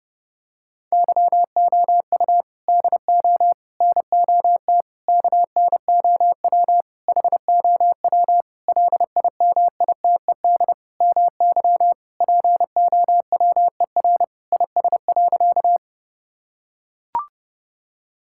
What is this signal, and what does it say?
Morse code, heard as a keyed tone, about 30 wpm, 700 Hz. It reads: YOU DO NOT KNOW HOW LIMITED MY POWER IS.
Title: Sentences from Top 1000 Words - Review Part 6 - 30wpm